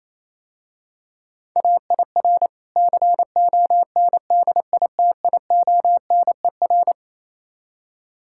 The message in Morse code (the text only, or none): AIR CONDITIONER